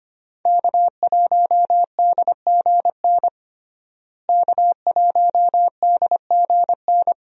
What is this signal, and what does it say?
Morse code, heard as a keyed tone, about 25 wpm, 700 Hz. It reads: K1DGN K1DGN
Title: USA Call Signs - 5 Characters - Repeated 2x - 25wpm